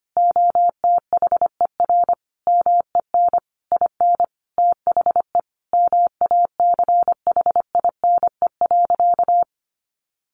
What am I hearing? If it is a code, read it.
Morse code, heard as a keyed tone, about 25 wpm, 700 Hz. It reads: OTHER MEN IN THE MACHINE.
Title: A Sound Of Thunder 25wpm